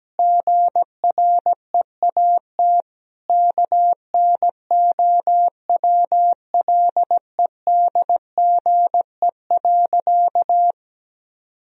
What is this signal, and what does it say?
Morse code, heard as a keyed tone, about 17 wpm, 700 Hz. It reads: GREAT KNOWLEDGE.
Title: Sentences from Top 1000 Words - Review Part 3 - 17wpm